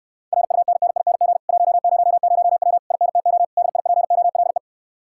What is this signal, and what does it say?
Morse code, heard as a keyed tone, about 69 wpm, 700 Hz. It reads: RUNNING 100W INTO DIPOLE